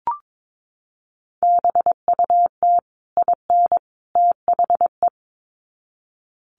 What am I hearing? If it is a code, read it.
Morse code, heard as a keyed tone, about 22 wpm, 700 Hz. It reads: BUT IN THE
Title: Sets of 3 Words 22wpm